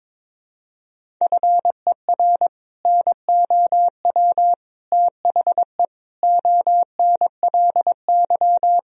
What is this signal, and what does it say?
Morse code, heard as a keyed tone, about 22 wpm, 700 Hz. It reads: FER NOW THE ONLY
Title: Sentences from Top 200 Words and Abbreviations 22wpm